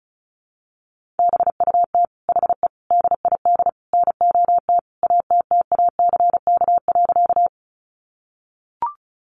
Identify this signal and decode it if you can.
Morse code, heard as a keyed tone, about 34 wpm, 700 Hz. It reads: BUT HE DID NOT ATTACK.
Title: Sentences from Top 1500 Words 35wpm